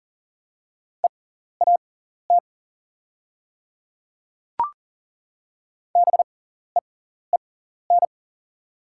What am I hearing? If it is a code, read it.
Morse code, heard as a keyed tone, about 39 wpm, spaced as for 14 wpm, 700 Hz. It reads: EAT BEEN
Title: 6x Character Spacing - Top 500 Words - 40wpm